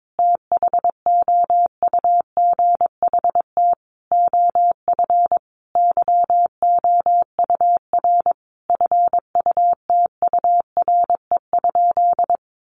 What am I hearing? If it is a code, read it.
Morse code, heard as a keyed tone, about 22 wpm, 700 Hz. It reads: THOUGHT OF YOUR FUTURE?
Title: Sentences from Top 700 Words 22wpm